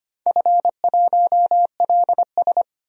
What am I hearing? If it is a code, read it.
Morse code, heard as a keyed tone, about 25 wpm, 700 Hz. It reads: F1LS
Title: DXCC-100 Call Signs - 4 Characters - 25wpm